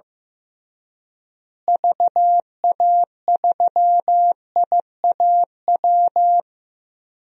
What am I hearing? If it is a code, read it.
Morse code, heard as a keyed tone, about 16 wpm, 700 Hz. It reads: VA3IAW
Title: DXCC-100 Call Signs - 6 Characters - Repeated 3x - 15wpm